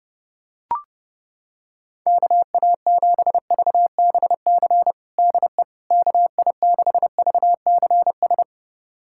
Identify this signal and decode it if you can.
Morse code, heard as a keyed tone, about 30 wpm, 700 Hz. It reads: KA7VBC DE KI6VCS
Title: USA Callsign DE Callsign - 6 Characters - 30wpm